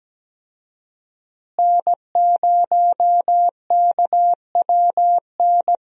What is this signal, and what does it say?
Morse code, heard as a keyed tone, about 17 wpm, 700 Hz. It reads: N0KWN
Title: USA Call Signs - 5 Characters - Repeated 2x - 17wpm